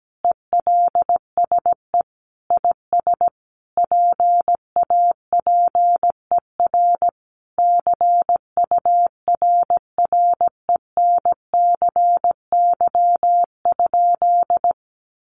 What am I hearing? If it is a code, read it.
Morse code, heard as a keyed tone, about 17 wpm, 700 Hz. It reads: ELSE IS PAPER CURRENCY?
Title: Sentences from Top 1500 Words 17wpm